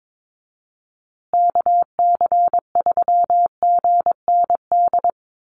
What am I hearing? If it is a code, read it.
Morse code, heard as a keyed tone, about 22 wpm, 700 Hz. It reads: KC3GND
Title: USA Call Signs Repeated 2x 22wpm